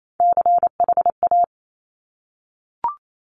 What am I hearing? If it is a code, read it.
Morse code, heard as a keyed tone, about 28 wpm, 700 Hz. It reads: CHA
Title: Top Three Letter Combinations in Words 28wpm